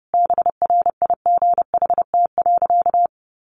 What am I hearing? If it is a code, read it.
Morse code, heard as a keyed tone, about 30 wpm, 700 Hz. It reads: BRIGHT.
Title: Sentences from Top 1500 Words 30wpm